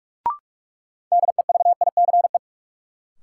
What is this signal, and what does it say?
Morse code, heard as a keyed tone, about 47 wpm, 700 Hz. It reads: DEVICE
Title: Sentences from Top 1500 Words - New Word Drill Alt - 45wpm